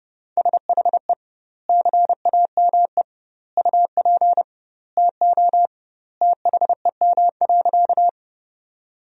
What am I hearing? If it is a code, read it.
Morse code, heard as a keyed tone, about 32 wpm, 700 Hz. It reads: SHE CAME UP TO THEM.
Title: 2x Word-Spacing - Sentences from Top 200 Words - 30wpm